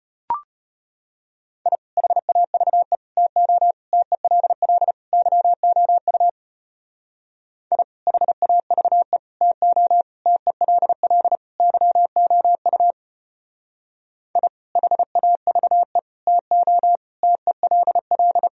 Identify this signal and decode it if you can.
Morse code, heard as a keyed tone, about 33 wpm, 700 Hz. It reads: I HAVE TO TELL YOU I HAVE TO TELL YOU I HAVE TO TELL